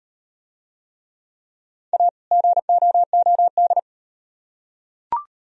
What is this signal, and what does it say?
Morse code, heard as a keyed tone, about 38 wpm, 700 Hz. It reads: A GOOD